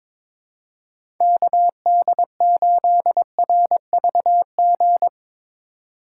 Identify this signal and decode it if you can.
Morse code, heard as a keyed tone, about 22 wpm, 700 Hz. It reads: KD8RVG
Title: USA Call Signs - 4x Character Spacing - 22wpm